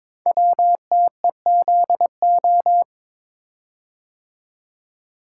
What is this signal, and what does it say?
Morse code, heard as a keyed tone, about 22 wpm, 700 Hz. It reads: WTEZO
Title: Code Groups - 5 Letters - 22wpm